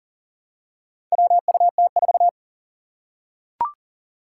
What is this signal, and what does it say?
Morse code, heard as a keyed tone, about 40 wpm, 700 Hz. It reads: WUT4
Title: Four Challenging Letter-Number Combinations 40wpm